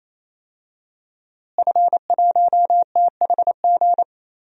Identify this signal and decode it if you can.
Morse code, heard as a keyed tone, about 28 wpm, 700 Hz. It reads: F1THG
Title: DXCC-100 Call Signs - 5 Characters - Repeated 3x - 28wpm